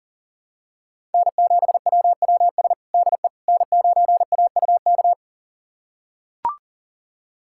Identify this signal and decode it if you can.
Morse code, heard as a keyed tone, about 40 wpm, 700 Hz. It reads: N7WWS DE N9AUK
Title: USA Callsign DE Callsign - 5 Characters - 40wpm